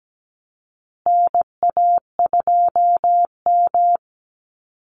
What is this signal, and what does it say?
Morse code, heard as a keyed tone, about 17 wpm, 700 Hz. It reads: NA2M
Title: Call Signs Repeated 2x - 17wpm